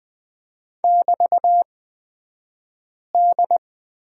Morse code, heard as a keyed tone, about 20 wpm, spaced as for 5 wpm, 700 Hz. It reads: <BT>D
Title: 223 Lesson 36d - BT - Two Characters 20wpm